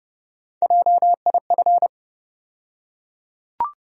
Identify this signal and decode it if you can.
Morse code, heard as a keyed tone, about 30 wpm, 700 Hz. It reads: JIF
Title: USA Call Sign Postfixes - 3 Letters 30wpm